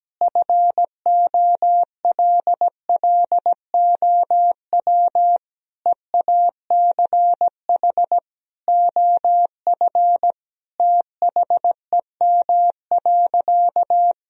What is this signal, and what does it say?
Morse code, heard as a keyed tone, about 17 wpm, 700 Hz. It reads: FOLLOW EACH OF THEM.
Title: Sentences from Top 1000 Words - Review Part 1 - 17wpm